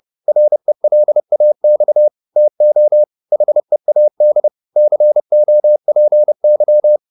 Morse code, heard as a keyed tone, about 30 wpm, 600 Hz. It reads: RELAX TO HEAD COPY